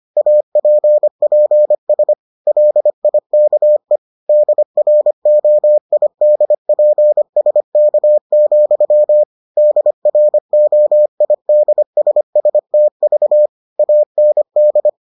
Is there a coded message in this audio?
Yes: Morse code from 0.2 to 14.9 s, heard as a keyed tone, about 25 wpm, 600 Hz. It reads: APPS LIKE DROIDPSK, DROIDSSTV AND